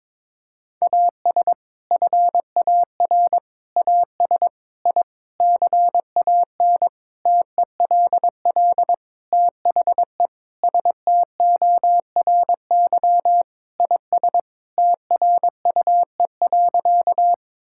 Morse code, heard as a keyed tone, about 22 wpm, 700 Hz. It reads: AS FAR AS I CAN TELL THE STORY IS TRUE.
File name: Sentences with 7-Word Phrases - 22wpm